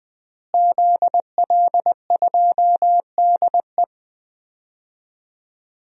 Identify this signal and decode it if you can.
Morse code, heard as a keyed tone, about 20 wpm, 700 Hz. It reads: ZL2DE
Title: DXCC-100 Call Signs - 5 Characters - Repeated 3x - 20wpm